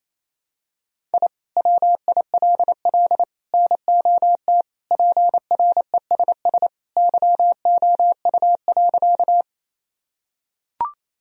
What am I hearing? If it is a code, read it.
Morse code, heard as a keyed tone, about 28 wpm, 700 Hz. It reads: I WILL NOT PRESS YOU.
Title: Sentences from Top 1000 Words - Review Part 1 - 28wpm